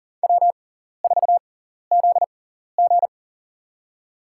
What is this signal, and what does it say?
Morse code, heard as a keyed tone, about 40 wpm, spaced as for 14 wpm, 700 Hz. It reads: W4ZG
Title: USA Call Signs - 6x Character Spacing - 40wpm